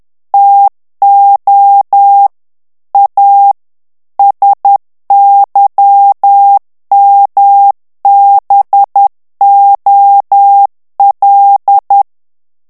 Morse code, heard as a keyed tone, about 10 wpm, 800 Hz. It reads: TO A SYMBOL
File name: ANGLAIS texte clair 3 vitesse 10 mots minute